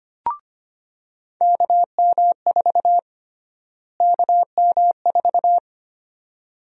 Morse code, heard as a keyed tone, about 25 wpm, 700 Hz. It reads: KM4 KM4
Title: USA Call Sign Prefixes - Repeated 2x - 25wpm